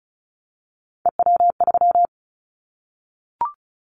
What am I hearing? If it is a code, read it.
Morse code, heard as a keyed tone, about 36 wpm, 700 Hz. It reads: EW3
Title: Callsign Prefixes for Top 50 DXCC - 35wpm